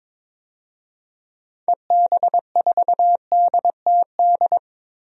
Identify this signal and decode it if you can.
Morse code, heard as a keyed tone, about 22 wpm, 700 Hz. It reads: EB4DTD
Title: DXCC-100 Call Signs - 6 Characters - Repeated 3x - 22wpm